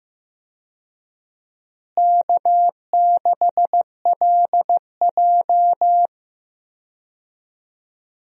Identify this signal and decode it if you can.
Morse code, heard as a keyed tone, about 15 wpm, 700 Hz. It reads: K6LJ
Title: USA Call Signs - 4 Characters - Repeated 3x - 15wpm